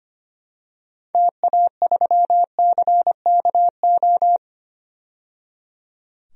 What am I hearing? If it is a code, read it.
Morse code, heard as a keyed tone, about 25 wpm, 690 Hz. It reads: TA3CKO
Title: DXCC-100 Call Signs - 6 Characters - 25wpm